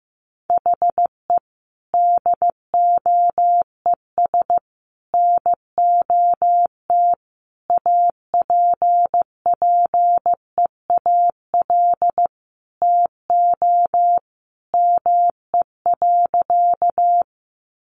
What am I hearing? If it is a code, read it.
Morse code, heard as a keyed tone, about 15 wpm, 700 Hz. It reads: HE DOES NOT APPEAL TO ME.
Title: Sentences from Top 1300 Words 15wpm